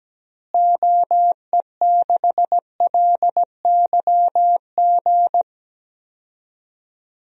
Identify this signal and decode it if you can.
Morse code, heard as a keyed tone, about 17 wpm, 700 Hz. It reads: OE6LYG
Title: DXCC-100 Call Signs - 6 Characters - Repeated 2x - 17wpm